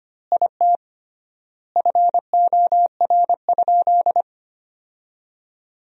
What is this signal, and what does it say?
Morse code, heard as a keyed tone, about 25 wpm, 700 Hz. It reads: IT FOR?
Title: 3x Word-Spacing - Sentences from Top 100 Words - 25wpm